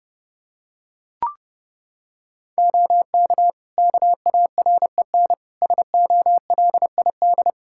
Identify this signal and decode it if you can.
Morse code, heard as a keyed tone, about 30 wpm, 700 Hz. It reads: OK KAREN SOLID
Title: Practice QSO - Exchange 3 - More Info - 30wpm